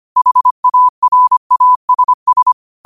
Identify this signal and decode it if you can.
Morse code, heard as a keyed tone, about 25 wpm, 1000 Hz. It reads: HARASS
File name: 25wpm